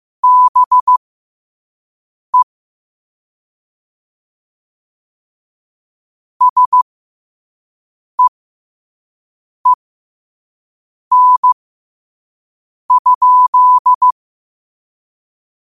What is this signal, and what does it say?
Morse code, heard as a keyed tone, about 15 wpm, spaced as for 5 wpm, 1000 Hz. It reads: BE SEEN?
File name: qotdfiles / 5wpm